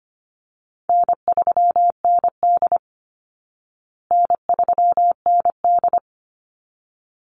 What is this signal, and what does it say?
Morse code, heard as a keyed tone, about 25 wpm, 700 Hz. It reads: N3ND N3ND
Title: Easy Call Signs Repeated 2x - 25wpm